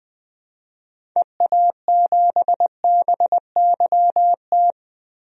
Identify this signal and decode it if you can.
Morse code, heard as a keyed tone, about 20 wpm, 700 Hz. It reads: EA7BYT
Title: DXCC-100 Call Signs - 6 Characters - 20wpm